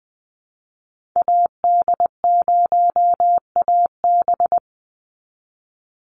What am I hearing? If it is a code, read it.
Morse code, heard as a keyed tone, about 20 wpm, 700 Hz. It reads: AD0AB
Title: Call Signs Repeated 3x - 20wpm